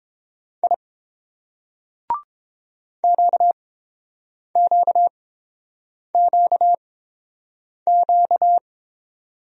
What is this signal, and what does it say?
Morse code, heard as a keyed tone, about 26 wpm, spaced as for 7 wpm, 700 Hz. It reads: I QQQQ
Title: Speed-Racing Single Letters 22wpm